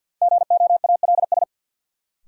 Morse code, heard as a keyed tone, about 49 wpm, 700 Hz. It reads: GOALS
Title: Sentences from Top 1000 Words - Review Part 6 - New Word Drill - 50wpm